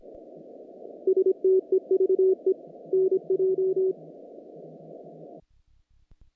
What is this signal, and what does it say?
Morse code, heard as a keyed tone, about 25 wpm, 380 Hz. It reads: STEVE NJ
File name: contests / naqp-cw-k5tr-op-k5pi- -1800